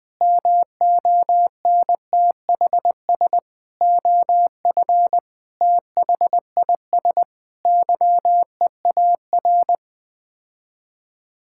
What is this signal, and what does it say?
Morse code, heard as a keyed tone, about 21 wpm, 700 Hz. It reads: MONTHS OF THIS YEAR
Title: Sets of 8 Words - 20wpm